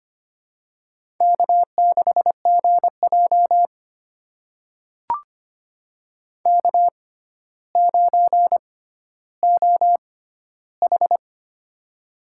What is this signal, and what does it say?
Morse code, heard as a keyed tone, about 25 wpm, spaced as for 13 wpm, 700 Hz. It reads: K6GJ K9OH
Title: USA Call Signs - 4 Characters - 6x Character Spacing - 25wpm